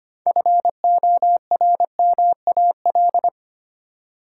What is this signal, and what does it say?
Morse code, heard as a keyed tone, about 25 wpm, 700 Hz. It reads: FORMAL
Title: Six Letter Words - 25wpm